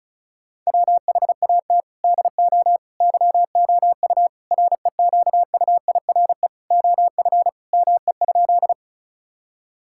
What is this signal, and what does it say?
Morse code, heard as a keyed tone, about 34 wpm, 700 Hz. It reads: WHAT DO YOU REQUIRE OF ME?
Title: Sentences from Top 1000 Words - Review Part 6 - 35wpm